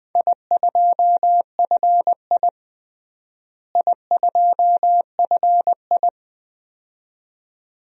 Morse code, heard as a keyed tone, about 20 wpm, 700 Hz. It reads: I2FI I2FI
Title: DXCC-100 Call Signs - 4 Characters - Repeated 2x - 20wpm